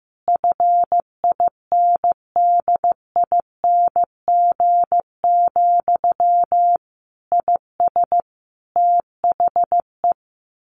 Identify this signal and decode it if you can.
Morse code, heard as a keyed tone, about 15 wpm, 700 Hz. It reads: FINDING, IS THE
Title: Sentences from Top 1100 Words 15wpm